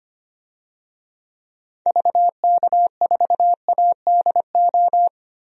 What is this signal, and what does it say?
Morse code, heard as a keyed tone, about 25 wpm, 700 Hz. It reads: VK4ADO
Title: DXCC-100 Call Signs - 6 Characters - 25wpm